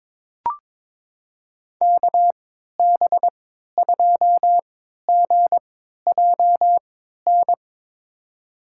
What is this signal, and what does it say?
Morse code, heard as a keyed tone, about 22 wpm, spaced as for 12 wpm, 700 Hz. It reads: KB2GJN
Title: USA Call Signs - 3x Character Spacing - 22wpm